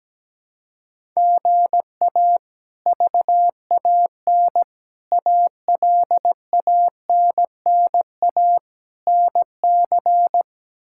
Morse code, heard as a keyed tone, about 17 wpm, 700 Hz. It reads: GA VAN ALANNA NC
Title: K1USN SST QSO Practice - 17wpm